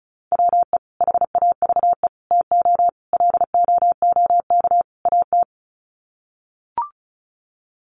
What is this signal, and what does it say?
Morse code, heard as a keyed tone, about 34 wpm, 700 Hz. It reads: WE HAVE TO LOOK AT
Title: Sets of 5 Words Encore2 35wpm